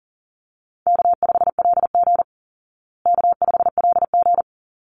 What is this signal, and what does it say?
Morse code, heard as a keyed tone, about 40 wpm, 700 Hz. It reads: K5LG K5LG
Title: Call Signs Repeated 3x - 40wpm